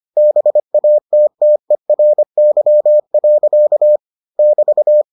Morse code, heard as a keyed tone, about 25 wpm, 600 Hz. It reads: BATTERY. <BT>